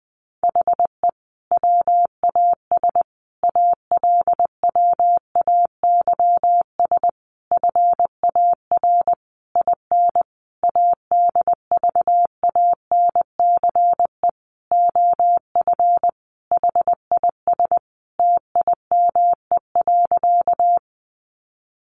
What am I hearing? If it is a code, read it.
Morse code, heard as a keyed tone, about 20 wpm, 700 Hz. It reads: HE WAS ALWAYS FAR IN ADVANCE OF HIS TIME.
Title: Sentences from Top 1400 Words 20wpm